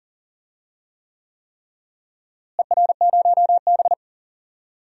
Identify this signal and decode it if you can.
Morse code, heard as a keyed tone, about 39 wpm, 700 Hz. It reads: ER0B
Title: DXCC-100 Call Signs - 4 Characters - Repeated 3x - 40wpm